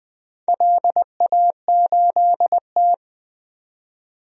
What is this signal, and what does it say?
Morse code, heard as a keyed tone, about 21 wpm, 700 Hz. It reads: LA8T
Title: DXCC-100 Call Signs - 4 Characters - 20wpm